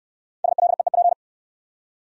Morse code, heard as a keyed tone, about 63 wpm, 700 Hz. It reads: SLEEP